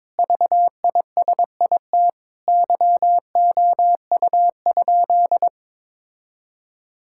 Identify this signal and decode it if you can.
Morse code, heard as a keyed tone, about 22 wpm, 700 Hz. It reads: VISIT YOU?
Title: Sentences from Top 1000 Words - Review Part 6 - 22wpm